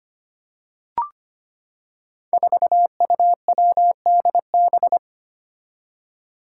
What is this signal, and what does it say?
Morse code, heard as a keyed tone, about 25 wpm, 700 Hz. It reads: 4UWDB
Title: Five Challenging Letter-Number Combinations 25wpm